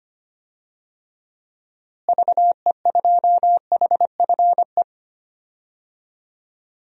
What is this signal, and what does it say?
Morse code, heard as a keyed tone, about 25 wpm, 700 Hz. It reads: VE2HFE
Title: DXCC-100 Call Signs - 6 Characters - Repeated 2x - 25wpm